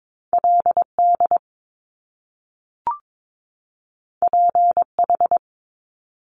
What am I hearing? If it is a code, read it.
Morse code, heard as a keyed tone, about 22 wpm, 700 Hz. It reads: LD PH
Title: Top Two Letter Combinations in Words 22wpm